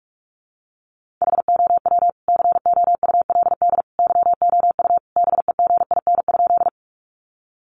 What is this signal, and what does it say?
Morse code, heard as a keyed tone, about 44 wpm, 700 Hz. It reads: HOW COULD YOU BEGIN?
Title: Sentences from Top 300 Words 45wpm